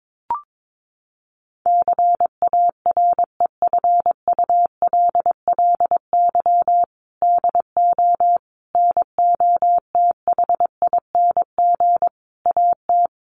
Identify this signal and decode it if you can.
Morse code, heard as a keyed tone, about 22 wpm, 700 Hz. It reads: CAREFULLY DO NOTHING AT
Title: Sentences from Top 1300 Words 22wpm